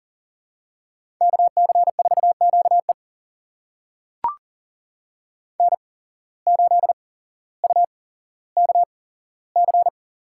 Morse code, heard as a keyed tone, about 37 wpm, spaced as for 17 wpm, 700 Hz. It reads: KC4QE N8UKC
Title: USA Call Signs - 5 Characters - 8x Character Spacing - 40wpm